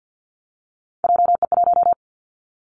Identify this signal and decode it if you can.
Morse code, heard as a keyed tone, about 50 wpm, 700 Hz. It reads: JE1
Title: Callsign Prefixes for Top 25 DXCC - 50wpm